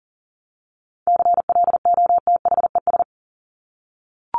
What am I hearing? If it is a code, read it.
Morse code, heard as a keyed tone, about 40 wpm, 700 Hz. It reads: CLOTHES
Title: Top 1751-2000 Words 40wpm